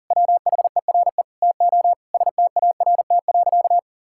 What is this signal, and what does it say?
Morse code, heard as a keyed tone, about 40 wpm, 700 Hz. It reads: WHERE TO START.
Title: Sentences from Top 1000 Words - Review Part 5 - 40wpm